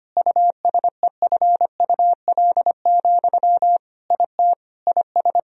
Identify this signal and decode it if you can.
Morse code, heard as a keyed tone, about 25 wpm, 700 Hz. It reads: USEFUL, IT IS
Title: Sentences from Top 1000 Words - Review Part 4 - 25wpm